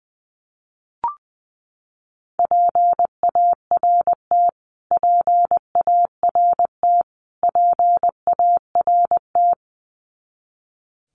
Morse code, heard as a keyed tone, about 20 wpm, 700 Hz. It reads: PART PART PART
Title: Top 300 Words Repeated 20wpm